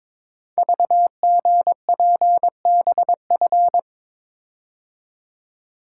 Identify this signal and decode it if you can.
Morse code, heard as a keyed tone, about 23 wpm, 700 Hz. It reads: VGPBF
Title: Code Groups - 5 Letters - 22wpm